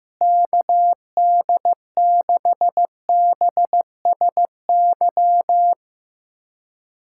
Morse code, heard as a keyed tone, about 15 wpm, 700 Hz. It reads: KD6BSY
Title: USA Callsign DE Callsign - 6 Characters - 15wpm